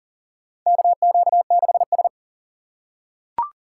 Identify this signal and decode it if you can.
Morse code, heard as a keyed tone, about 40 wpm, 700 Hz. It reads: KQ6S
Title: USA Call Signs - 4 Characters - 4x Character Spacing - 40wpm